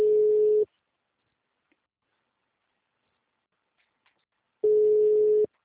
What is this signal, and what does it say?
Ringback tone (Europe, 425 Hz).
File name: out- -2002- - - .910